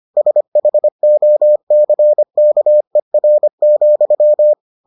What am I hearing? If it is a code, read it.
Morse code, heard as a keyed tone, about 25 wpm, 600 Hz. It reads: SHOCKER,